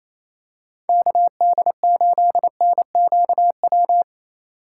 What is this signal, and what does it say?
Morse code, heard as a keyed tone, about 28 wpm, 700 Hz. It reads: KD8NQW
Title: USA Call Signs - 2x Character Spacing - 28wpm